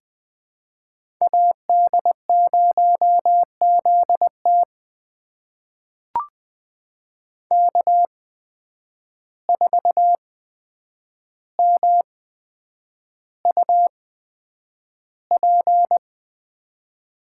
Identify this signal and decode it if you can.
Morse code, heard as a keyed tone, about 21 wpm, spaced as for 9 wpm, 700 Hz. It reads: AD0ZT K4MUP
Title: USA Call Signs - 8x Character Spacing - 20wpm